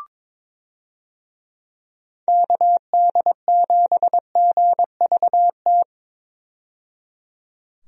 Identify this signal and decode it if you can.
Morse code, heard as a keyed tone, about 22 wpm, 700 Hz. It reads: KD7GVT
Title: USA Call Signs - 6 Characters - 22wpm